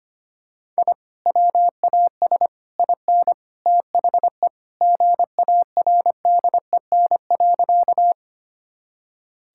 Morse code, heard as a keyed tone, about 25 wpm, 700 Hz. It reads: I WAS IN THE GARDEN.
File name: practice / 25wpm / Sentences with 3-Word Phrases - 25wpm